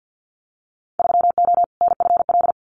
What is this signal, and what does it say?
Morse code, heard as a keyed tone, about 50 wpm, 700 Hz. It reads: 3O NFL
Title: Winter Field Day Exchanges 50wpm